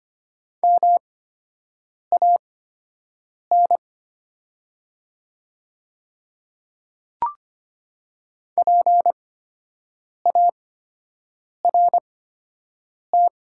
Morse code, heard as a keyed tone, about 25 wpm, spaced as for 7 wpm, 700 Hz. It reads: MAN PART
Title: 8x Character Spacing - Top 200 Words - 25wpm